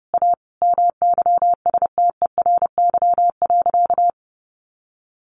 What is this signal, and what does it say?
Morse code, heard as a keyed tone, about 30 wpm, 700 Hz. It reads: A MYSTERY.
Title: A Sound Of Thunder 30wpm